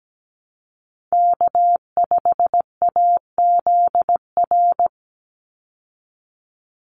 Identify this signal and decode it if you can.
Morse code, heard as a keyed tone, about 17 wpm, 700 Hz. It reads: K5AZR
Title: Call Signs Repeated 3x - 17wpm